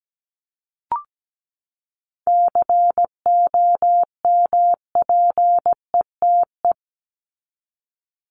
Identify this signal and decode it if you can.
Morse code, heard as a keyed tone, about 17 wpm, 700 Hz. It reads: COMPETE
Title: Top 1751-2000 Words 17wpm